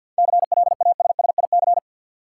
Morse code, heard as a keyed tone, about 51 wpm, 700 Hz. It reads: CLASSIC
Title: Sentences from Top 1000 Words - New Word Drill - 50wpm